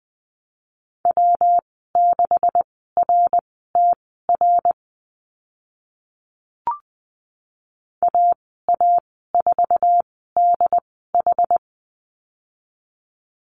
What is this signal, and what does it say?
Morse code, heard as a keyed tone, about 20 wpm, spaced as for 15 wpm, 700 Hz. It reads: W6RTR AA4DH
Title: 140 Lesson 23g - Letter f - Call Signs 20wpm